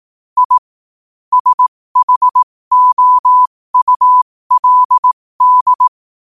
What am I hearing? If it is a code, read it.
Morse code, heard as a keyed tone, about 18 wpm, spaced as for 15 wpm, 1000 Hz. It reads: I SHOULD
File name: qotdfiles / 15wpm